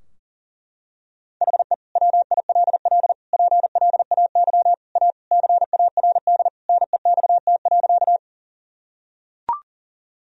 Morse code, heard as a keyed tone, about 40 wpm, 700 Hz. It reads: HE WILL PLAY A CARD NEXT.
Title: Sentences from Top 1000 Words - Review Part 4 - 40wpm